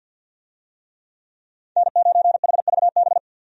Morse code, heard as a keyed tone, about 50 wpm, 700 Hz. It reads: N9HVB
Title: USA Call Signs - 5 Characters - 50wpm